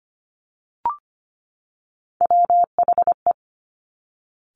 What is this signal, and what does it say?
Morse code, heard as a keyed tone, about 25 wpm, 700 Hz. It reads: WHE
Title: Top Three Letter Combinations in Words 25wpm